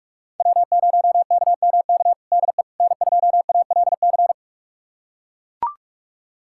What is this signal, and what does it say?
Morse code, heard as a keyed tone, about 41 wpm, 700 Hz. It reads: W0KMK DE N2ALC
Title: USA Callsign DE Callsign - 5 Characters - 45wpm